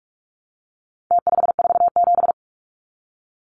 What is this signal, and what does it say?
Morse code, heard as a keyed tone, about 44 wpm, 700 Hz. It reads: T547
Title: Numbers - Contest Style - 4-Digit 45wpm